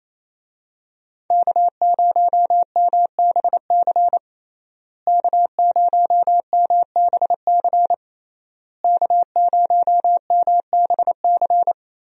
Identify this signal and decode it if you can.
Morse code, heard as a keyed tone, about 28 wpm, 700 Hz. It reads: K0MBC K0MBC K0MBC
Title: USA Call Signs - 5 Characters - Repeated 3x - 28wpm